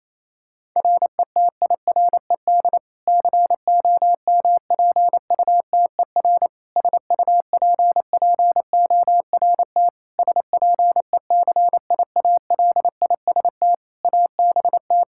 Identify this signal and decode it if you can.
Morse code, heard as a keyed tone, about 28 wpm, 700 Hz. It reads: RETIRED COMPUTER SUPPORT SPECIALIST ABT
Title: Practice QSO - Exchange 3 - More Info - 28wpm